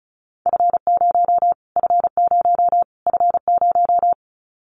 Morse code, heard as a keyed tone, about 34 wpm, 700 Hz. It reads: F0 F0 F0
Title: Callsign Prefixes for Top 25 DXCC - Repeated 3x - 35wpm